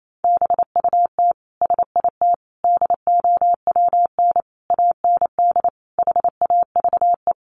Morse code, heard as a keyed tone, about 28 wpm, 700 Hz. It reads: BUT SIT DOWN AND HAVE
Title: Sentences from Top 600 Words 28wpm